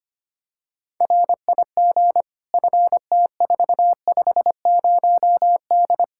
Morse code, heard as a keyed tone, about 25 wpm, 700 Hz. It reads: RIG FT450D
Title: Practice QSO - Exchange 2 - Info - 25wpm